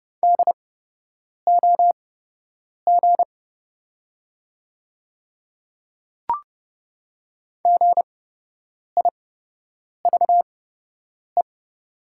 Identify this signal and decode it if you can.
Morse code, heard as a keyed tone, about 30 wpm, spaced as for 8 wpm, 700 Hz. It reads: DOG GIVE
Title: 8x Character Spacing - Top 500 Words - 30wpm